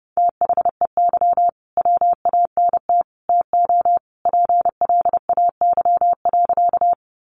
Morse code, heard as a keyed tone, about 30 wpm, 700 Hz. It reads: THEY WANT TO PLAY.
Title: Easy Phrases 30wpm